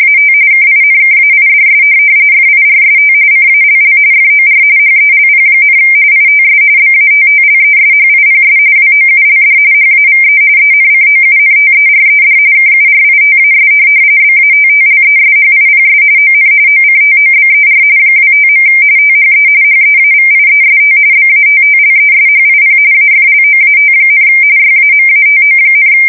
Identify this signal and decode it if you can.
RTTY, about 45.45 baud. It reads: AT KINKAIDS FIRST CALL FOR HELP, ITS BATTLESHIPS WOULD HAVE ENCOUNTERED KURITAS BATTLESHIPS ON OCTOBER 25. THE HEAVIEST BATTLESHIP AFLOAT, YAMATO,
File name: fsk170 / USS-New-Jersey